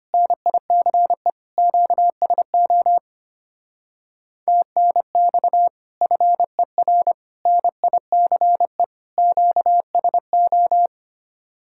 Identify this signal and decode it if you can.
Morse code, heard as a keyed tone, about 32 wpm, 700 Hz. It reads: NICE QSO TNX FER NICE QSO